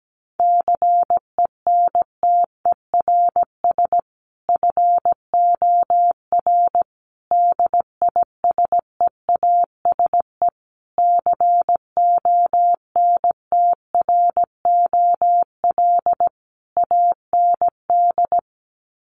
Morse code, heard as a keyed tone, about 17 wpm, 700 Hz. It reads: CENTERS FOR DISEASE CONTROL AND
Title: Sets of 5 Words Encore2 17wpm